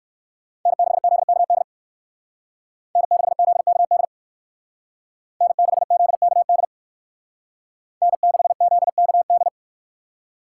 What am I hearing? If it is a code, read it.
Morse code, heard as a keyed tone, about 54 wpm, 700 Hz. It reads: N6ZKD N6ZKD N6ZKD N6ZKD
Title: Speed Racing - USA 1x3 Call Signs 45wpm